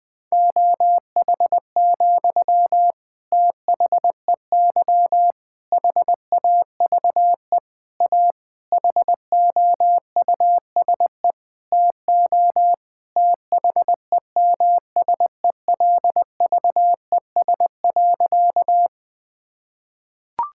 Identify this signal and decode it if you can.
Morse code, heard as a keyed tone, about 20 wpm, 700 Hz. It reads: OH, THEY HAVE A HOUSE TO THEMSELVES.
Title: Sentences from Top 1000 Words - Review Part 5 - 20wpm